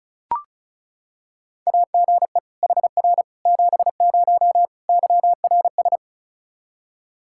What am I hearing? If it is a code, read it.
Morse code, heard as a keyed tone, about 34 wpm, 700 Hz. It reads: AGE HR 70 YRS
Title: Practice QSO - Exchange 3 - More Info - 35wpm